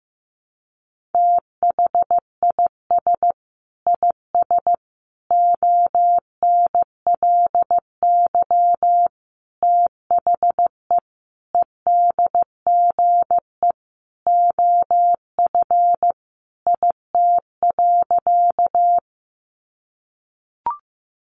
Morse code, heard as a keyed tone, about 15 wpm, 700 Hz. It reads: THIS IS ONLY THE EDGE OF IT.
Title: Sentences from Top 1300 Words 15wpm